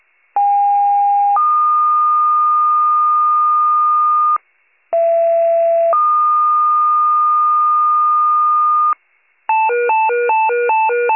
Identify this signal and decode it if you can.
Two-tone page, heard as a one-second tone, then a three-second tone, about 810 then 1220 Hz, about 670 then 1220 Hz. It is heard from 0.4 to 8.9 s.